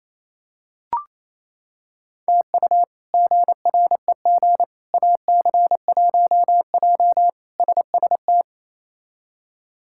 Morse code, heard as a keyed tone, about 28 wpm, 700 Hz. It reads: TU GREG AC1J SST
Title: K1USN SST QSO Practice - 28wpm